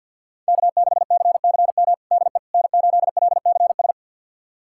Morse code, heard as a keyed tone, about 49 wpm, 700 Hz. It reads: K6CCM DE N8LCS
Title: USA Callsign DE Callsign - 5 Characters - 50wpm